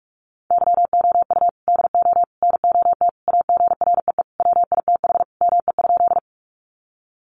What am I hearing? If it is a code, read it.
Morse code, heard as a keyed tone, about 44 wpm, 700 Hz. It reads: YOU DO NOT AGREE WITH ME?
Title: Sentences from Top 1100 Words 45wpm